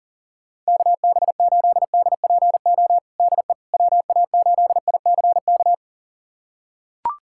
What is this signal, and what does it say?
Morse code, heard as a keyed tone, about 40 wpm, 700 Hz. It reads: KB8DPO DE WA8ICK
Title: USA Callsign DE Callsign - 6 Characters - 40wpm